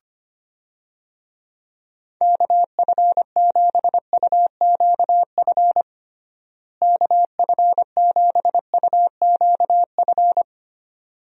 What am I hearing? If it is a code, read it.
Morse code, heard as a keyed tone, about 25 wpm, 700 Hz. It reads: KF7UQF KF7UQF
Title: USA Call Signs - 6 Characters - Repeated 2x - 25wpm